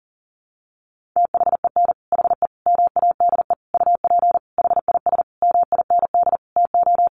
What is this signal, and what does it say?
Morse code, heard as a keyed tone, about 40 wpm, 700 Hz. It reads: THEN HE MADE UP HIS MIND TO
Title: Sentences from Top 1400 Words 40wpm